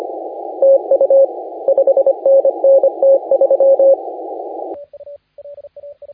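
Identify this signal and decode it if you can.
Morse code, heard as a keyed tone, about 25 wpm, 560 Hz, with radio noise. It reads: TU 5NN T3
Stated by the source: Morse contest audio recorded by LA4C during the SAC CW 2018 contest.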